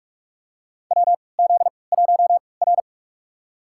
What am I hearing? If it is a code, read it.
Morse code, heard as a keyed tone, about 46 wpm, spaced as for 25 wpm, 700 Hz. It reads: WZ1R
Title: USA Call Signs - 3x Character Spacing - 45wpm